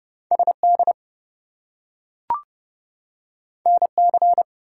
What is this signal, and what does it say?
Morse code, heard as a keyed tone, about 30 wpm, 700 Hz. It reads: SD NC
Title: US States and Canadian Provinces 30wpm